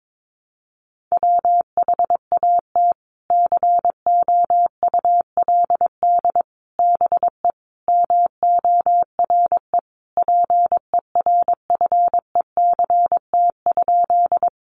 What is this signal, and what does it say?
Morse code, heard as a keyed tone, about 22 wpm, 700 Hz. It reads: WHAT COULD BE MORE PERFECT?
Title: Sentences from Top 1000 Words 22wpm